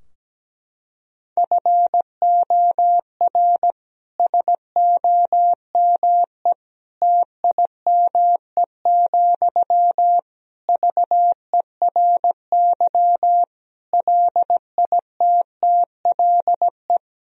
Morse code, heard as a keyed tone, about 17 wpm, 700 Hz. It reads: FOR SOME TIME, VERY LITTLE
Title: Sentences from Top 1000 Words - Review Part 4 - 17wpm